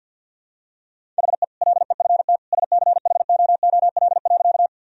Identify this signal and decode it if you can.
Morse code, heard as a keyed tone, about 50 wpm, 700 Hz. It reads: HE LEFT SCHOOL.
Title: Sentences from Top 1000 Words - Review Part 1 - 50wpm